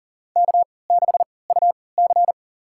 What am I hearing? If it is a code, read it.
Morse code, heard as a keyed tone, about 40 wpm, spaced as for 23 wpm, 700 Hz. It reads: K6UC
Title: USA Call Signs - 3x Character Spacing - 40wpm